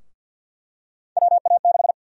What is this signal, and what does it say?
Morse code, heard as a keyed tone, about 52 wpm, 700 Hz. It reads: WA6
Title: USA Call Sign Prefixes - 50wpm